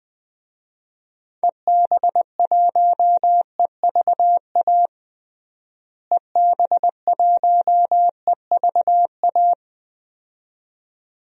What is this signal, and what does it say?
Morse code, heard as a keyed tone, about 20 wpm, 700 Hz. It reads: EB1EVA EB1EVA
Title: DXCC-100 Call Signs - 6 Characters - Repeated 2x - 20wpm